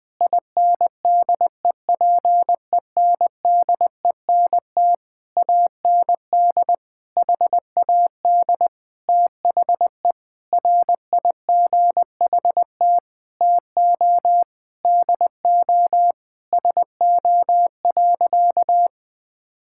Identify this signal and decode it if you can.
Morse code, heard as a keyed tone, about 19 wpm, 700 Hz. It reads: INDEPENDENT AND HAD THE RIGHT TO DO SO.
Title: Sentences from Top 1000 Words - Review Part 6 - 20wpm